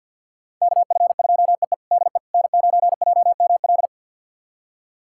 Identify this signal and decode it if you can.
Morse code, heard as a keyed tone, about 48 wpm, 690 Hz. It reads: KF2EE DE N9JML
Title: USA Callsign DE Callsign - 5 Characters - 50wpm